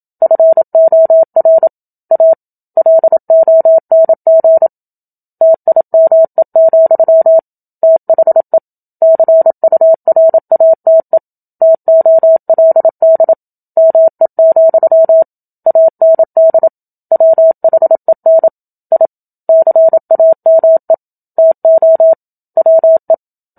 Morse code, heard as a keyed tone, about 27 wpm, 650 Hz. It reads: FOR A LONG TIME, THE CURATE TOLD ME, AND WHEN I CAME TO WE
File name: Book Two / 01-Chapter 1 - 27 WPM